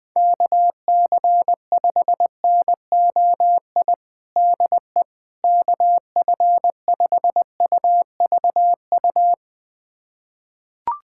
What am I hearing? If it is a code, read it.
Morse code, heard as a keyed tone, about 19 wpm, 700 Hz. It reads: KC5NOI DE KF5UVU
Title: USA Callsign DE Callsign - 6 Characters - 20wpm